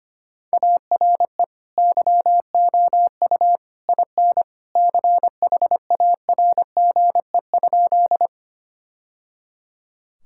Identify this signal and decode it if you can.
Morse code, heard as a keyed tone, about 25 wpm, 700 Hz. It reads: ARE YOU IN CHARGE?
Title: Sentences from Top 1000 Words - Review Part 6 - 25wpm